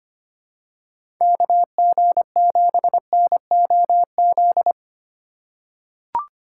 Morse code, heard as a keyed tone, about 25 wpm, 700 Hz. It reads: KG7NOZ
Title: USA Call Signs - 6 Characters - 3x Character Spacing - 25wpm